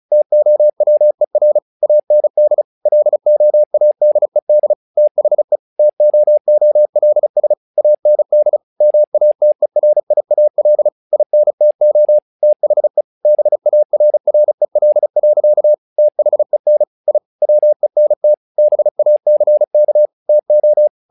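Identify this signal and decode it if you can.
Morse code, heard as a keyed tone, about 34 wpm, 600 Hz. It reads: TOWER AND LOADED THE TOOLS AND MATERIAL INTO THE BARREL. THEN I WENT BACK TO
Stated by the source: Morse practice text "Tower Trouble" sent at 35 WPM, from the SKCC - Straight Key Century Club learning center audio files.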